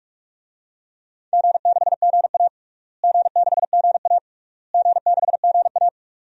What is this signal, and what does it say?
Morse code, heard as a keyed tone, about 46 wpm, 700 Hz. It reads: G6GA G6GA G6GA
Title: DXCC-100 Call Signs - 4 Characters - Repeated 3x - 45wpm